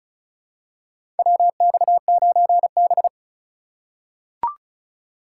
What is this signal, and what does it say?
Morse code, heard as a keyed tone, about 35 wpm, 700 Hz. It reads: WX9B
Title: USA Call Signs - 4 Characters - 2x Character Spacing - 35wpm